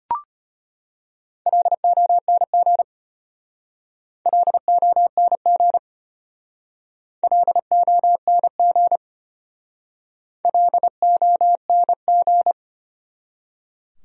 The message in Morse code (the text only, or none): LONG LONG LONG LONG